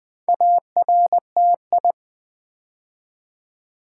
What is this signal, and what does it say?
Morse code, heard as a keyed tone, about 20 wpm, 700 Hz. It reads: ARTI
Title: Build A Word - 2 Letters At A Time - 6-Letter Words - 20wpm